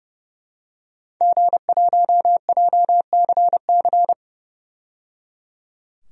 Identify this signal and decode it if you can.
Morse code, heard as a keyed tone, about 30 wpm, 700 Hz. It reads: G1JCC
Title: DXCC-100 Call Signs - 5 Characters - 30wpm